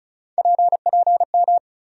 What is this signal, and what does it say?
Morse code, heard as a keyed tone, about 34 wpm, 700 Hz. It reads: PPM
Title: USA Call Sign Postfixes - 3 Letters 35wpm